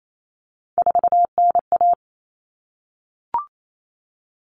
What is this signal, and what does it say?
Morse code, heard as a keyed tone, about 28 wpm, 700 Hz. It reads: 4NA
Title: Common Cut Numbers - 3-Digit 28wpm